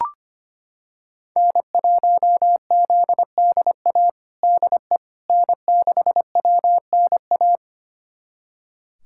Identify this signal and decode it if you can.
Morse code, heard as a keyed tone, about 25 wpm, 700 Hz. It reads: N1ZDA DE N6WNA
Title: USA Callsign DE Callsign - 5 Characters - 25wpm